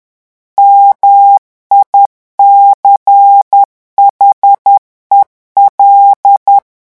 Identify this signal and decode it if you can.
Morse code, heard as a keyed tone, about 10 wpm, 800 Hz. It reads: MICHEL
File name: morse mp3 / revision francais 10 mots minute